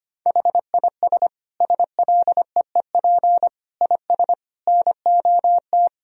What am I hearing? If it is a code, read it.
Morse code, heard as a keyed tone, about 25 wpm, 700 Hz. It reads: HIS SLEEP IS NOT
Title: Sentences from Top 1000 Words - Review Part 4 - 25wpm